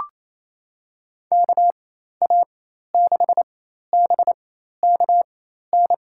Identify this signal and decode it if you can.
Morse code, heard as a keyed tone, about 27 wpm, spaced as for 13 wpm, 700 Hz. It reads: KA6BKN
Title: USA Call Signs - 6 Characters - 4x Character Spacing - 28wpm